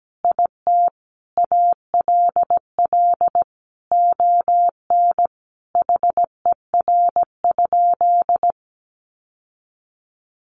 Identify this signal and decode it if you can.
Morse code, heard as a keyed tone, about 17 wpm, 700 Hz. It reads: IT ALL ON HER?
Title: Sentences from Top 800 Words 17wpm